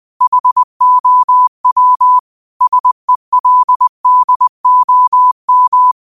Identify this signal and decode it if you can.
Morse code, heard as a keyed tone, about 20 wpm, 1000 Hz. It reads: HOW SELDOM
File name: qotdfiles / 20wpm